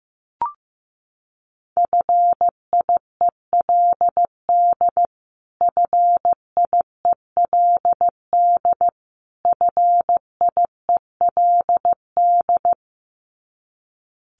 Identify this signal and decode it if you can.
Morse code, heard as a keyed tone, about 15 wpm, 700 Hz. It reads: FIELD FIELD FIELD
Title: Top 500 Words Repeated 15wpm